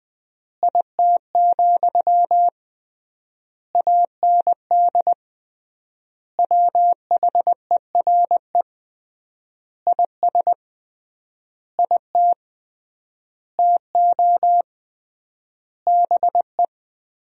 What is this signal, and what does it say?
Morse code, heard as a keyed tone, about 21 wpm, 700 Hz. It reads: IT, AND WHERE IS IT TO BE
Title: 3x Word-Spacing - Sentences from Top 300 Words - 20wpm